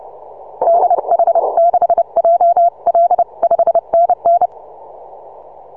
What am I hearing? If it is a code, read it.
Morse code, heard as a keyed tone, about 29 wpm, 680 Hz. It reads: DH6JL 5NN
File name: soundclips / TZ6BB 6MCW